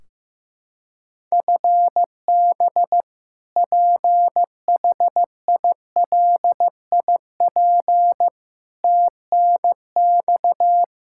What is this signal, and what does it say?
Morse code, heard as a keyed tone, about 16 wpm, 700 Hz. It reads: FB PHILIP TNX
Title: Practice QSO - Exchange 3 - More Info - 15wpm